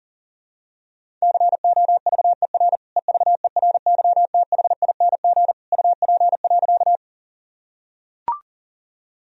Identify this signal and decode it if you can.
Morse code, heard as a keyed tone, about 40 wpm, 700 Hz. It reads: COVER EVERYTHING UP.
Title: Sentences from Top 1000 Words - Review Part 1 - 40wpm